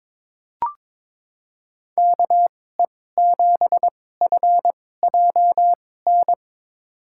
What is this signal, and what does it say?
Morse code, heard as a keyed tone, about 21 wpm, spaced as for 16 wpm, 700 Hz. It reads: KE7FJN
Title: USA Call Signs - 6 Characters - 2x Character Spacing - 22wpm